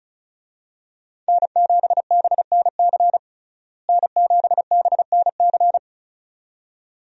Morse code, heard as a keyed tone, about 34 wpm, 700 Hz. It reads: N7BNC N7BNC
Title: USA Call Signs - 5 Characters - Repeated 2x - 35wpm